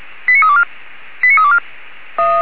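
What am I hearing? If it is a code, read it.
Five-tone selective call (ZVEI), heard as quick runs of five tones, tones of about 70 ms. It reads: 89324 89324
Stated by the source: ZVEI five-tone selective call received on an SDR.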